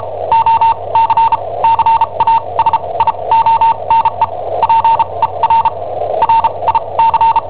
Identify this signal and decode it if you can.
Morse code, heard as a keyed tone, about 34 wpm, spaced as for 25 wpm, 910 Hz. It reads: OCCASIONE PER RIC
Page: Cassetta esercizi preparata da I7OHP per IW5ELP